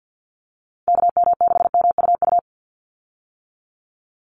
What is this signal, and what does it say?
Morse code, heard as a keyed tone, about 50 wpm, 700 Hz. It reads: KM6MUU
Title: USA Call Signs 50wpm